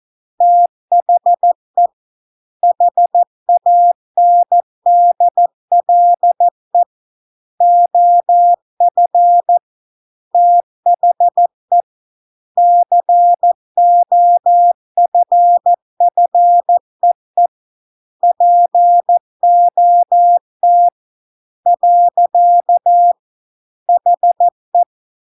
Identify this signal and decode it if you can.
Morse code, heard as a keyed tone, about 14 wpm, 700 Hz. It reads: THE HANDLE OF THE COFFEE POT . HE